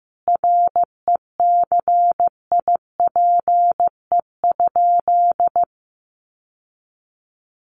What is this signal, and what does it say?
Morse code, heard as a keyed tone, about 15 wpm, 700 Hz. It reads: RECIPE?
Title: Sentences from Top 1200 Words 15wpm